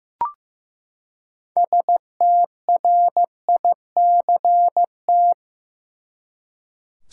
Morse code, heard as a keyed tone, about 15 wpm, 700 Hz. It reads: STRICT
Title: Six Letter Words - 15wpm